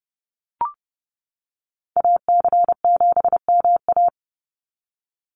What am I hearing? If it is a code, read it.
Morse code, heard as a keyed tone, about 30 wpm, 700 Hz. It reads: AC7MA
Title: NA SOTA Activator Callsigns 30wpm